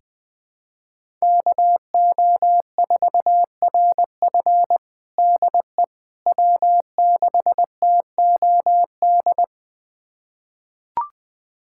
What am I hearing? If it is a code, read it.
Morse code, heard as a keyed tone, about 20 wpm, 700 Hz. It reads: KO4RF DE W6TOD
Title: USA Callsign DE Callsign - 5 Characters - 20wpm